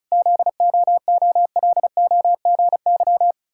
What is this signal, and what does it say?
Morse code, heard as a keyed tone, about 36 wpm, 700 Hz. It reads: ZOOLOGY
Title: Pangram Challenge v2 - 35wpm